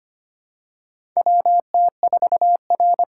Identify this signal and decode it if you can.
Morse code, heard as a keyed tone, about 25 wpm, 700 Hz. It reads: WT4R
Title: USA Call Signs - 4 Characters - 2x Character Spacing - 25wpm